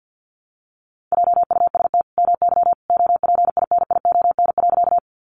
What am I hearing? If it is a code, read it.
Morse code, heard as a keyed tone, about 50 wpm, 700 Hz. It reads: JUST MY OPINION.
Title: Sentences from Top 1200 Words 50wpm